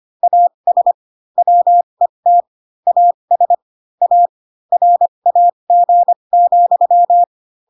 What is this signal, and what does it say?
Morse code, heard as a keyed tone, about 25 wpm, spaced as for 22 wpm, 700 Hz. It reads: AS WET AS A RAG,
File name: Pinocchio / Pinocchio ch6 18 25 ik1hge